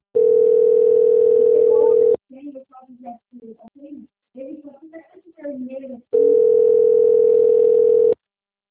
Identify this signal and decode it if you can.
Ringback tone (North America), 440 + 480 Hz, with speech.
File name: out- - - - - .7577